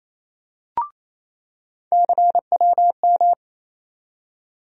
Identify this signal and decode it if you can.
Morse code, heard as a keyed tone, about 28 wpm, 700 Hz. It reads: CWM
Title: USA Call Sign Postfixes - 3 Letters 28wpm